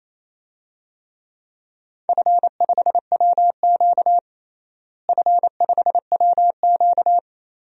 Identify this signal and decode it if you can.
Morse code, heard as a keyed tone, about 28 wpm, 700 Hz. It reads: F5WQ F5WQ
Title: DXCC-100 Call Signs - 4 Characters - Repeated 3x - 28wpm